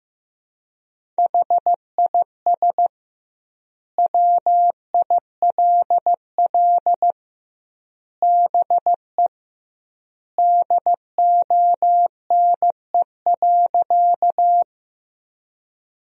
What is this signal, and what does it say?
Morse code, heard as a keyed tone, about 16 wpm, 700 Hz. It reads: HIS WILL BE DONE.
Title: 2x Word-Spacing - Sentences from Top 500 Words - 15wpm